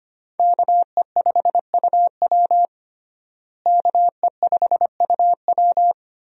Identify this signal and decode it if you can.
Morse code, heard as a keyed tone, about 25 wpm, 700 Hz. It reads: KE5UW KE5UW
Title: USA Call Signs - 5 Characters - Repeated 2x - 25wpm